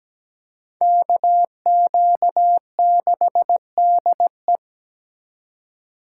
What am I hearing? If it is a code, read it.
Morse code, heard as a keyed tone, about 20 wpm, 700 Hz. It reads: KQ6DE
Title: Speed Racing - USA 2x2 Call Signs 17wpm